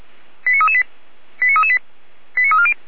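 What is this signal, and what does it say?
Five-tone selective call (ZVEI), heard as quick runs of five tones, tones of about 70 ms. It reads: 89338 89338 89340
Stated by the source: ZVEI five-tone selective call 89104 received on SDR, logged by the Feuerwehr Orlamünde fire brigade server.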